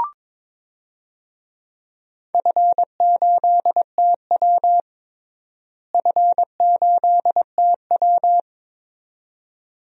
Morse code, heard as a keyed tone, about 21 wpm, 700 Hz. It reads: F8TW F8TW
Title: DXCC-100 Call Signs - 4 Characters - Repeated 2x - 22wpm